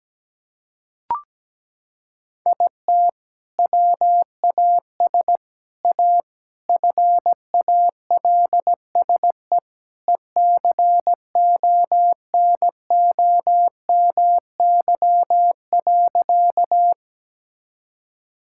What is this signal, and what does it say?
Morse code, heard as a keyed tone, about 17 wpm, 700 Hz. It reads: IT WAS A FALSE ECONOMY.
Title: Sentences from Top 1600 Words 17wpm